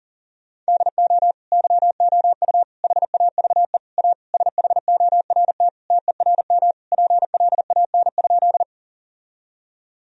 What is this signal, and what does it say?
Morse code, heard as a keyed tone, about 37 wpm, 700 Hz. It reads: DO YOU HAVE A SHORT TERM PLAN?
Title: Sentences from Top 1000 Words - Review Part 4 - 40wpm